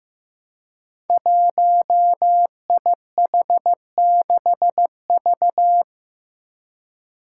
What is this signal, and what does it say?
Morse code, heard as a keyed tone, about 15 wpm, 700 Hz. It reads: 1IH6V
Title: Five Challenging Letter-Number Combinations 15wpm